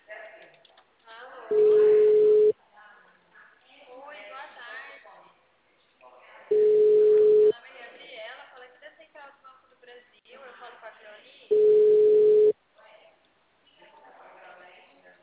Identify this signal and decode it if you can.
Ringback tone (Europe, 425 Hz), with speech.
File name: out- -2044- - - .37662